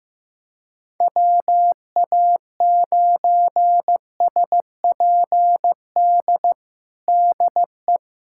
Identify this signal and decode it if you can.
Morse code, heard as a keyed tone, about 16 wpm, 700 Hz. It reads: WA9SPD DE
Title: USA Callsign DE Callsign - 6 Characters - 15wpm